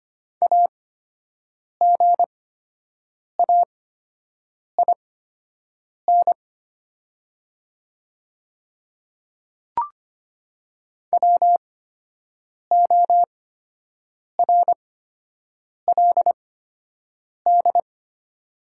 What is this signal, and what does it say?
Morse code, heard as a keyed tone, about 25 wpm, spaced as for 7 wpm, 700 Hz. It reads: AGAIN WORLD
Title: 8x Character Spacing - Top 200 Words - 25wpm